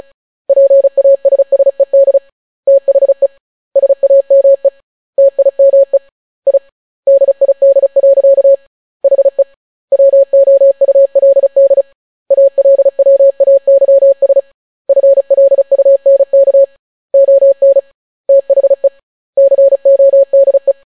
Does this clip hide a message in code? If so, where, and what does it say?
Yes: Morse code from 0.5 to 20.7 s, heard as a keyed tone, about 36 wpm, 550 Hz. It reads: PASSED THE SAME TIME I DID. HE WOULD ALWAYS FLUNK ON THE CODE